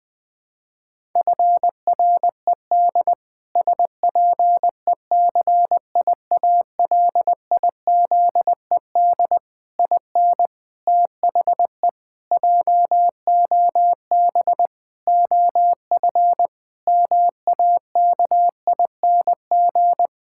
Morse code, heard as a keyed tone, about 20 wpm, 700 Hz. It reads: FRED SPECIALIZED IN THE JOB OF MAKING
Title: Pangram Challenge v2 - 20wpm